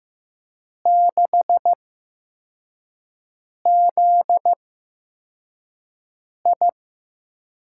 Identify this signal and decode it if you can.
Morse code, heard as a keyed tone, about 15 wpm, spaced as for 4 wpm, 700 Hz. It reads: 6ZI
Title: USA Call Signs - 4 Characters - 8x Character Spacing - 15wpm